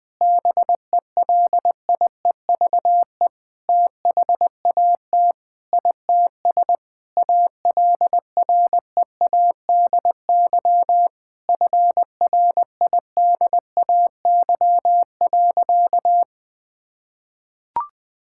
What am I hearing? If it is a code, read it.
Morse code, heard as a keyed tone, about 20 wpm, 700 Hz. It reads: BELIEVE THAT ITS ALREADY FRIDAY.
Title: Sentences with 7-Word Phrases - Spoken Beginning - 20wpm